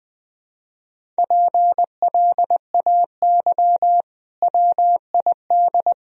Morse code, heard as a keyed tone, about 20 wpm, 700 Hz. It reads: PLAY WID
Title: Sentences from Top 200 Words and Abbreviations 20wpm